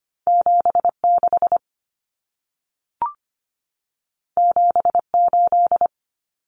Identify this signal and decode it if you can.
Morse code, heard as a keyed tone, about 25 wpm, 700 Hz. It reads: 76 78
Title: Numbers - 2-Digit 25wpm